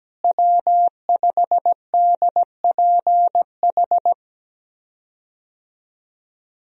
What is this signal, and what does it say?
Morse code, heard as a keyed tone, about 17 wpm, 700 Hz. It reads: W5DPH
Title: USA Call Signs - 5 Characters - Repeated 2x - 17wpm